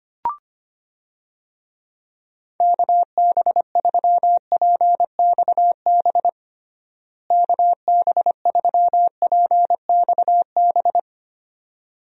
Morse code, heard as a keyed tone, about 25 wpm, 700 Hz. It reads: KB3PXB KB3PXB
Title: USA Call Signs - 6 Characters - Repeated 2x - 25wpm